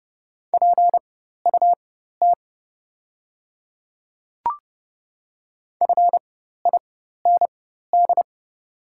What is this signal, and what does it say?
Morse code, heard as a keyed tone, about 30 wpm, spaced as for 14 wpm, 690 Hz. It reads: PUT FIND
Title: 4x Character Spacing - Top 300 Words - 30wpm